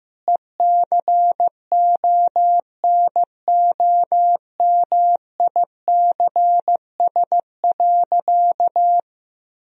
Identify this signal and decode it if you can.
Morse code, heard as a keyed tone, about 15 wpm, 700 Hz. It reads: ECONOMICS.
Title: Sentences from Top 1000 Words - Review Part 3 - 15wpm